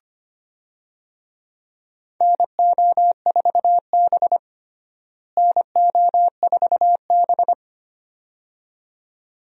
Morse code, heard as a keyed tone, about 25 wpm, 700 Hz. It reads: NO4B NO4B
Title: USA Call Signs - 4 Characters - Repeated 2x - 25wpm